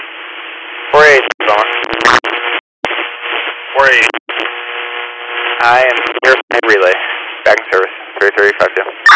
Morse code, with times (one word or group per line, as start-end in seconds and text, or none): none